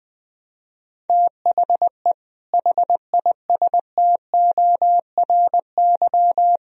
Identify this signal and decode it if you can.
Morse code, heard as a keyed tone, about 21 wpm, 700 Hz. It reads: THE HISTORY